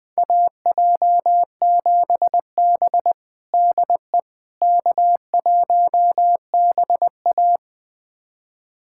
Morse code, heard as a keyed tone, about 21 wpm, 700 Hz. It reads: AJ7B DE K1BA
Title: USA Callsign DE Callsign - 4 Characters - 20wpm